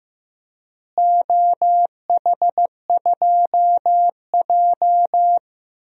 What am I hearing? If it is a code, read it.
Morse code, heard as a keyed tone, about 15 wpm, 700 Hz. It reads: OH2J
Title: DXCC-100 Call Signs - 4 Characters - Repeated 3x - 15wpm